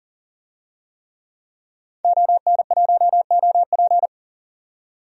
Morse code, heard as a keyed tone, about 40 wpm, 700 Hz. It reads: ON1OP
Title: DXCC-100 Call Signs - 5 Characters - 40wpm